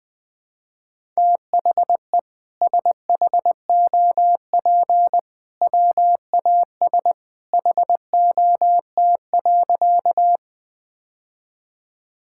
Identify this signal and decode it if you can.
Morse code, heard as a keyed tone, about 20 wpm, 700 Hz. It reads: THE SHOP WAS HOT.
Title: Sentences from Top 1600 Words 20wpm